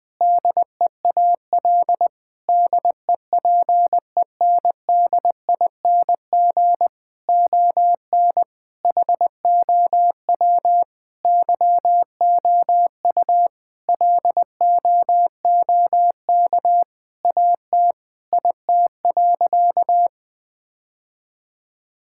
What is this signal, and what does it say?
Morse code, heard as a keyed tone, about 20 wpm, 700 Hz. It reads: DEAL DEPENDING ON HOW YOU LOOK AT IT.
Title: Sentences with 7-Word Phrases - 20wpm